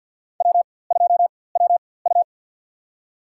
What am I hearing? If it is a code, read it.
Morse code, heard as a keyed tone, about 51 wpm, spaced as for 23 wpm, 690 Hz. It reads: W2WU
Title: USA Call Signs - 4x Character Spacing - 50wpm